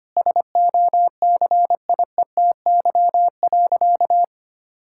Morse code, heard as a keyed tone, about 25 wpm, 700 Hz. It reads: SOCIETY.
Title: Sentences from Top 1000 Words - Review Part 6 - 25wpm